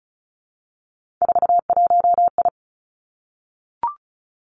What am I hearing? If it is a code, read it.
Morse code, heard as a keyed tone, about 36 wpm, 700 Hz. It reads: 41I
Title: Three Challenging Letter-Number Combinations 35wpm